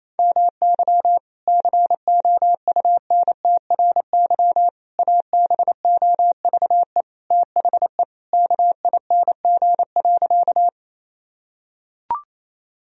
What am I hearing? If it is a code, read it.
Morse code, heard as a keyed tone, about 28 wpm, 700 Hz. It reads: MY COUNTRY ABOVE THE KING.
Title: Sentences from Top 1000 Words - Review Part 2 - 28wpm